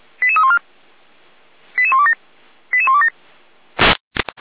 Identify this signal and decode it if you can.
Five-tone selective call (ZVEI), heard as quick runs of five tones, tones of about 70 ms. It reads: 88314 88127 88127